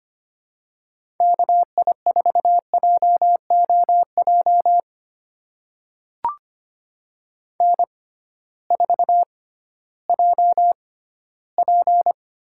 Morse code, heard as a keyed tone, about 25 wpm, spaced as for 15 wpm, 700 Hz. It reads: KI4JOJ N4JP
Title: USA Call Signs - 6x Character Spacing - 25wpm